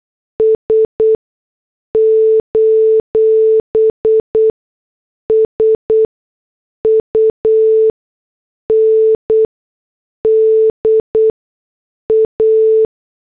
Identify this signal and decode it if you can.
Morse code, heard as a keyed tone, about 8 wpm, spaced as for 6 wpm, 440 Hz. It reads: S: SUNDA
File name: morse / headlines